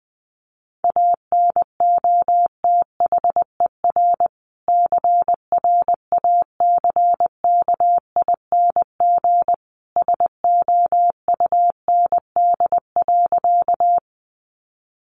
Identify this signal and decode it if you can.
Morse code, heard as a keyed tone, about 20 wpm, 700 Hz. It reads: ANOTHER CRACKING SOUND.
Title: A Sound Of Thunder 20wpm